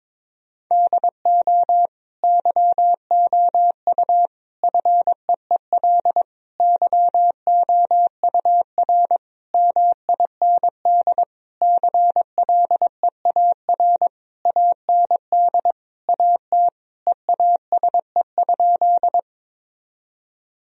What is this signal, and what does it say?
Morse code, heard as a keyed tone, about 22 wpm, 700 Hz. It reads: DO YOU FEEL YOUR MIND CLEAR AND AT EASE?
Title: Sentences from Top 1000 Words - Review Part 1 - 22wpm